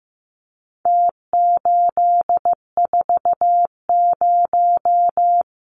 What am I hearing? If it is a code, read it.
Morse code, heard as a keyed tone, about 15 wpm, 700 Hz. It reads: T840
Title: Numbers - Contest Style - 4-Digit 15wpm